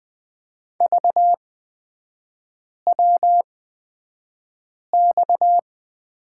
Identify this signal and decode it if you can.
Morse code, heard as a keyed tone, about 20 wpm, spaced as for 5 wpm, 690 Hz. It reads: VWX
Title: 229 Lesson 37e - Letter x - Three Characters 20wpm